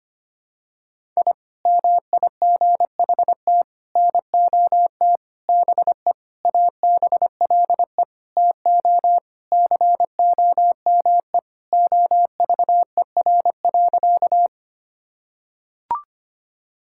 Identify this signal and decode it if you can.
Morse code, heard as a keyed tone, about 25 wpm, 700 Hz. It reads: I MIGHT NOT BE ABLE TO COME OVER.
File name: practice / 25wpm / Sentences with 4-Word Phrases - 25wpm